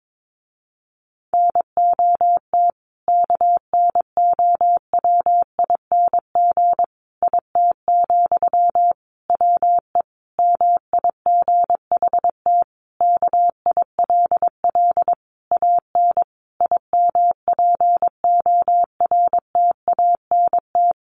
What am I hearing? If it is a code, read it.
Morse code, heard as a keyed tone, about 22 wpm, 700 Hz. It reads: NOT KNOWING IT, WE MIGHT KILL AN IMPORTANT
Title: A Sound Of Thunder 22wpm